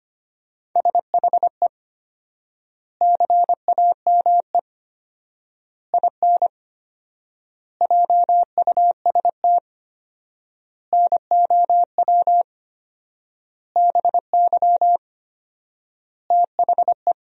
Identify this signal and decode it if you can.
Morse code, heard as a keyed tone, about 25 wpm, 700 Hz. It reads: SHE CAME IN JUST NOW BY THE
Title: 4x Word-Spacing - Sentences from Top 300 Words - 25wpm